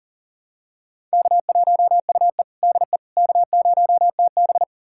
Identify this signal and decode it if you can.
Morse code, heard as a keyed tone, about 37 wpm, 700 Hz. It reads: K1UE DE K0TB
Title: USA Callsign DE Callsign - 4 Characters - 40wpm